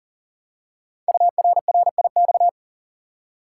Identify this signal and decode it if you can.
Morse code, heard as a keyed tone, about 42 wpm, 690 Hz. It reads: URRIX
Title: Code Groups - 5 Letters - 40wpm